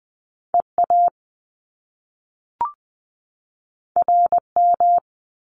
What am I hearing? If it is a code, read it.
Morse code, heard as a keyed tone, about 20 wpm, 700 Hz. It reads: EA RM
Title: Top Two Letter Combinations in Words 20wpm